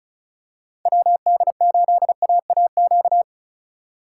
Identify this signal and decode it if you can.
Morse code, heard as a keyed tone, about 36 wpm, 700 Hz. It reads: WD8AAQ
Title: USA Call Signs - 6 Characters - 4x Character Spacing - 35wpm